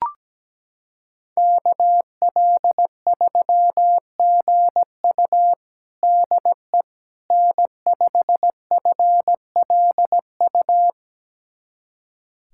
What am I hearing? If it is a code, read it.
Morse code, heard as a keyed tone, about 17 wpm, 700 Hz. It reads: KL3GU DE N5FLU
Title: USA Callsign DE Callsign - 5 Characters - 17wpm